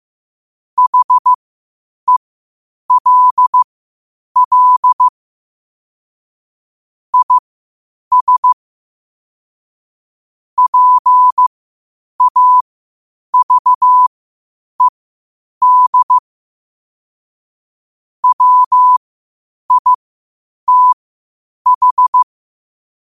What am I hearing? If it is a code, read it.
Morse code, heard as a keyed tone, about 15 wpm, spaced as for 8 wpm, 1000 Hz. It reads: HELL IS PAVED WITH